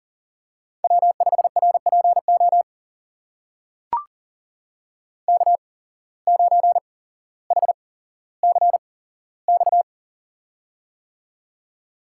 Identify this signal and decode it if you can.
Morse code, heard as a keyed tone, about 40 wpm, spaced as for 17 wpm, 700 Hz. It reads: W5RPO K9HCX
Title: USA Call Signs - 5 Characters - 8x Character Spacing - 40wpm